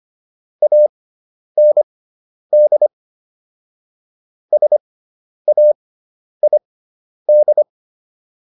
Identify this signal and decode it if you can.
Morse code, heard as a keyed tone, about 25 wpm, spaced as for 10 wpm, 600 Hz. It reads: AND SAID